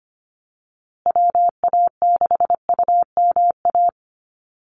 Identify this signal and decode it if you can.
Morse code, heard as a keyed tone, about 25 wpm, 700 Hz. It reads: WA6UMA
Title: USA Call Signs Repeated 3x 25wpm